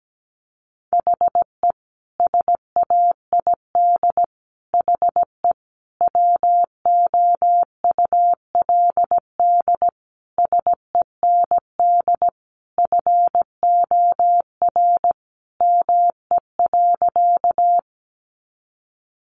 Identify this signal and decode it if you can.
Morse code, heard as a keyed tone, about 17 wpm, 700 Hz. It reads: HE SAID HE WOULD SEND FOR ME.
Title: Sentences from Top 1500 Words 17wpm